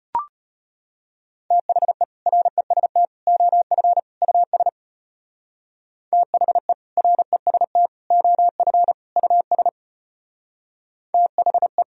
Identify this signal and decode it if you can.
Morse code, heard as a keyed tone, about 33 wpm, 700 Hz. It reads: THE REST OF US THE REST OF US THE